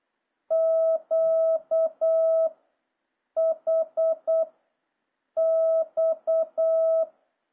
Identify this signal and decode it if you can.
Morse code, heard as a keyed tone, about 8 wpm, 650 Hz. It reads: Q H X